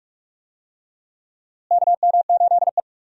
Morse code, heard as a keyed tone, about 44 wpm, 700 Hz. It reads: KM8E
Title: USA Call Signs - 4 Characters - Repeated 2x - 45wpm